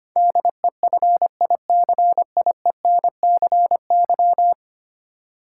Morse code, heard as a keyed tone, about 25 wpm, 700 Hz. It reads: DEFICIENCY
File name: Ten Letter Words - 25wpm